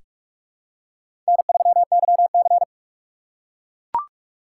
Morse code, heard as a keyed tone, about 43 wpm, 700 Hz. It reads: N3YC
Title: USA Call Signs - 4 Characters - 3x Character Spacing - 45wpm